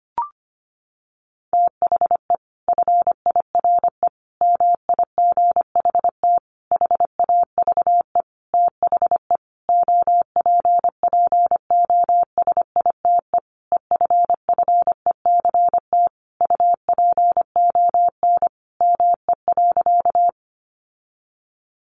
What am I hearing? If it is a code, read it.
Morse code, heard as a keyed tone, about 25 wpm, 700 Hz. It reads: THE FIRE MIGHT HAVE THE OPPOSITE EFFECT UPON ME.
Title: Sentences from Top 1300 Words 25wpm